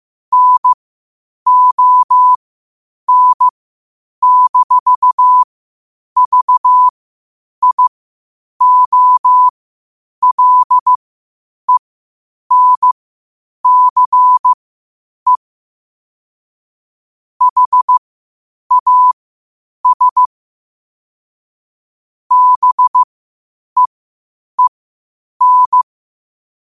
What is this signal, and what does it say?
Morse code, heard as a keyed tone, about 15 wpm, spaced as for 8 wpm, 1000 Hz. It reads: NON-VIOLENCE HAS BEEN
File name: qotdfiles / 8wpm